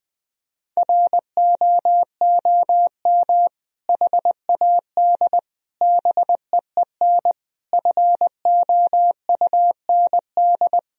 Morse code, heard as a keyed tone, about 20 wpm, 700 Hz. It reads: ROOM HAD BEEN FOUND
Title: Sentences from Top 1000 Words - Review Part 3 - 20wpm